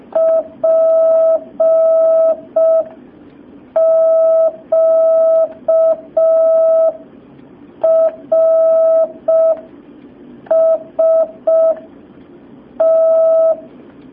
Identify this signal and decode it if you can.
Morse code, heard as a keyed tone, about 5 wpm, spaced as for 4 wpm, 650 Hz. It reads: PQRST
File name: jh8cbh / 2019cw